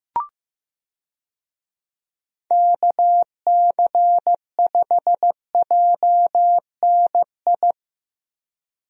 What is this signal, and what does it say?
Morse code, heard as a keyed tone, about 15 wpm, 700 Hz. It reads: KC5JNI
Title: USA Call Signs - 6 Characters - Repeated 3x - 15wpm